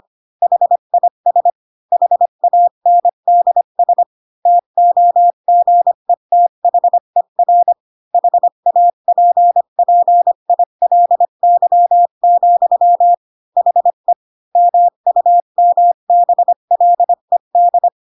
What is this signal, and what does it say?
Morse code, heard as a keyed tone, about 25 wpm, 700 Hz. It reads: HIS HANDS TOGETHER HAPPILY, HE MUMBLED
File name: Pinocchio / Pinocchio ch1 20 25 ik1hge